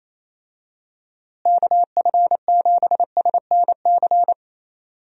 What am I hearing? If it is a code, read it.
Morse code, heard as a keyed tone, about 28 wpm, 700 Hz. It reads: KF7SNC
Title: USA Call Signs - 6 Characters - Repeated 2x - 28wpm